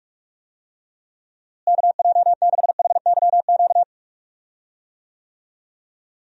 Morse code, heard as a keyed tone, about 46 wpm, 700 Hz. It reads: KJ6HYQ
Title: USA Call Signs - 6 Characters - Repeated 2x - 45wpm